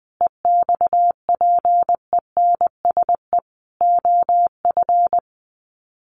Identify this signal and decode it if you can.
Morse code, heard as a keyed tone, about 20 wpm, 700 Hz. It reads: EXPENSE OF
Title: Sets of 4 Words 20wpm